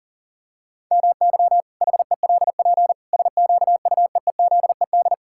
Morse code, heard as a keyed tone, about 37 wpm, 700 Hz. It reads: MY HELP SQUEEZED
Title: Pangram Challenge v2 - 40wpm